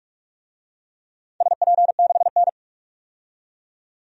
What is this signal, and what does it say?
Morse code, heard as a keyed tone, about 42 wpm, 700 Hz. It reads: SP6N
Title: DXCC-100 Call Signs - 4 Characters - 45wpm